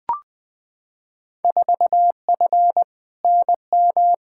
Morse code, heard as a keyed tone, about 20 wpm, 700 Hz. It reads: 4F NM
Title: Field Day Exchanges Encore - 20wpm